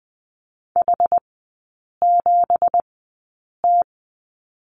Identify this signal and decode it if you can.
Morse code, heard as a keyed tone, about 20 wpm, spaced as for 8 wpm, 700 Hz. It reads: H7T
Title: 172 Lesson 28e - Number 9 - Three Characters 20wpm